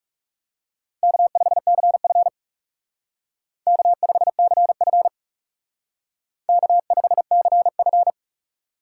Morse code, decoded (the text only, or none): K5CF K5CF K5CF